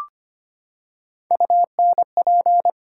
Morse code, heard as a keyed tone, about 25 wpm, 700 Hz. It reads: UNP
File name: Ten Letter Words - 25wpm